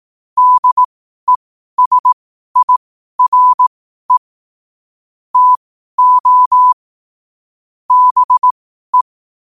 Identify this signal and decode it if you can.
Morse code, heard as a keyed tone, about 18 wpm, spaced as for 12 wpm, 1000 Hz. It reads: DESIRE TO BE
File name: qotdfiles / 12wpm